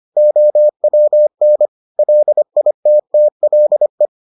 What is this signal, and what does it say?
Morse code, heard as a keyed tone, about 25 wpm, 600 Hz. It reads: OWN LITTLE